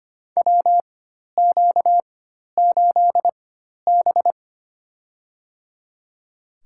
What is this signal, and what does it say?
Morse code, heard as a keyed tone, about 25 wpm, spaced as for 12 wpm, 700 Hz. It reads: WQ8B
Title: USA Call Signs - 4x Character Spacing - 25wpm